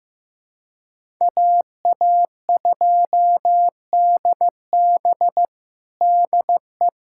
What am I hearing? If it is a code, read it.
Morse code, heard as a keyed tone, about 16 wpm, 700 Hz. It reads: AA2DB DE
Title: USA Callsign DE Callsign - 5 Characters - 15wpm